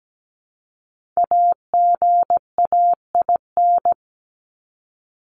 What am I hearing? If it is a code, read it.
Morse code, heard as a keyed tone, about 17 wpm, 700 Hz. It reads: AGAIN
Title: Top 200 Words 17wpm